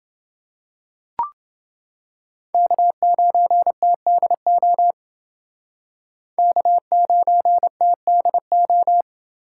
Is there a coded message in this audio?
Yes: Morse code from 2.5 to 9.0 s, heard as a keyed tone, about 28 wpm, 700 Hz. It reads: K9TDO K9TDO